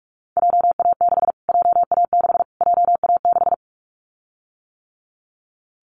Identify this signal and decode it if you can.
Morse code, heard as a keyed tone, about 44 wpm, 700 Hz. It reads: JA6 JA6 JA6
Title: Callsign Prefixes for Top 25 DXCC - Repeated 3x - 45wpm